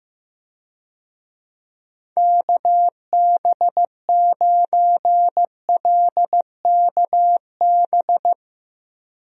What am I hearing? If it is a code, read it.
Morse code, heard as a keyed tone, about 16 wpm, 700 Hz. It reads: KB9LKB
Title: USA Call Signs - 6 Characters - 15wpm